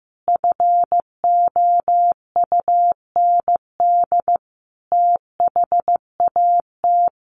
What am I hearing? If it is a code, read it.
Morse code, heard as a keyed tone, about 15 wpm, 700 Hz. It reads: FOUND THAT
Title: Sentences from Top 1200 Words 15wpm